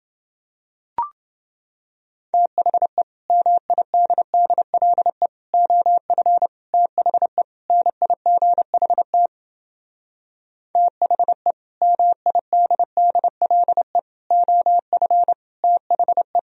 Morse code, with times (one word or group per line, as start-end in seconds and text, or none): 2.3-3.0 THE
3.3-5.3 MIDDLE
5.5-6.5 OF
6.7-7.4 THE
7.7-9.3 NIGHT
10.8-11.5 THE
11.8-14.0 MIDDLE
14.3-15.3 OF
15.6-16.4 THE